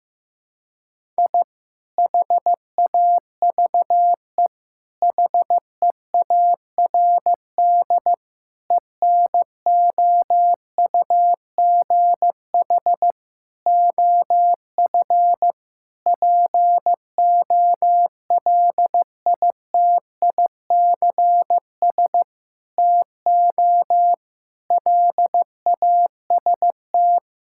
Morse code, heard as a keyed tone, about 15 wpm, 700 Hz. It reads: I HAVE HEARD ENOUGH OF POLITICS TO LAST
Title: Sentences from Top 1000 Words - Review Part 6 - 15wpm